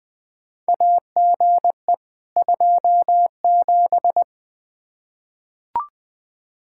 Morse code, heard as a keyed tone, about 20 wpm, 700 Hz. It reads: AGE 27
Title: K1USN SST QSO Practice 20wpm